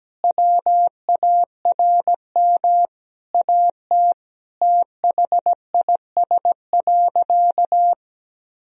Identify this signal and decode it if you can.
Morse code, heard as a keyed tone, about 17 wpm, 700 Hz. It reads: WARM AT THIS.
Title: Sentences from Top 1000 Words - Review Part 2 - 17wpm